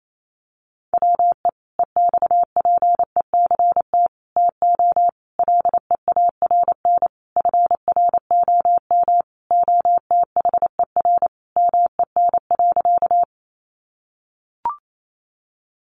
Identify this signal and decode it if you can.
Morse code, heard as a keyed tone, about 28 wpm, 700 Hz. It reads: WE EXPECT TO LEARN FROM OTHER MEN.
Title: Sentences from Top 1100 Words 28wpm